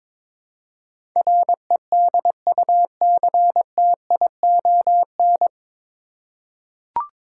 Morse code, heard as a keyed tone, about 22 wpm, 700 Hz. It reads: REDUCTION
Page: Morse Code Ninja practice 22wpm